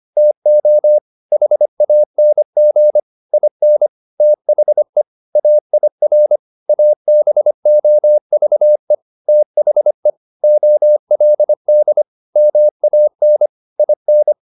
Morse code, heard as a keyed tone, about 25 wpm, 600 Hz. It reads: TO HANG IN THE AIR ABOVE THE OLD MAN IN